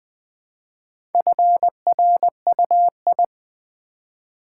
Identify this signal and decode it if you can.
Morse code, heard as a keyed tone, about 20 wpm, 700 Hz. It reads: FRUI
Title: Build A Word - 1 Letter At A Time - 5-Letter Words - 20wpm